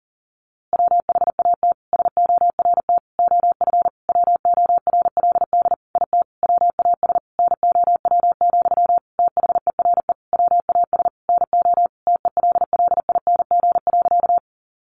Morse code, heard as a keyed tone, about 40 wpm, 700 Hz. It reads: WHAT SORT OF WORLD IT WAS NOW, THERE WAS NO TELLING.
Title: A Sound Of Thunder 40wpm